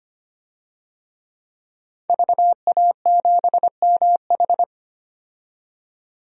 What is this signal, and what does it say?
Morse code, heard as a keyed tone, about 25 wpm, 700 Hz. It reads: VA7MH
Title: DXCC-100 Call Signs - 5 Characters - 25wpm